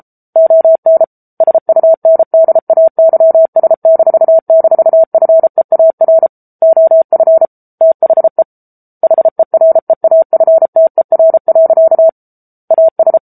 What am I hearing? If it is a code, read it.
Morse code, heard as a keyed tone, about 34 wpm, 650 Hz. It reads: ON SUNDAYS--FEAR OF THE HEREAFTER. AS
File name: Book Two / 07-Chapter 7 - 33 WPM